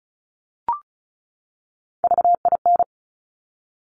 Morse code, heard as a keyed tone, about 36 wpm, 700 Hz. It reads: VIN
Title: Top Three Letter Combinations in Words 35wpm